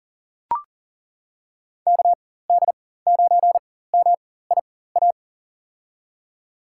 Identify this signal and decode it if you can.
Morse code, heard as a keyed tone, about 40 wpm, spaced as for 19 wpm, 700 Hz. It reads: KD9MIA
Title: USA Call Signs - 6 Characters - 4x Character Spacing - 40wpm